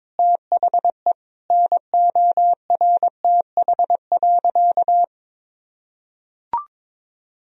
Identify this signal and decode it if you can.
Morse code, heard as a keyed tone, about 21 wpm, 700 Hz. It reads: THE NORTH.
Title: Sentences from Top 1000 Words - Review Part 1 - 22wpm